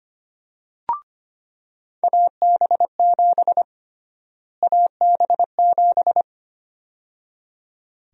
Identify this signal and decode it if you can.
Morse code, heard as a keyed tone, about 25 wpm, 700 Hz. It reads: AB7 AB7
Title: USA Call Sign Prefixes - Repeated 2x - 25wpm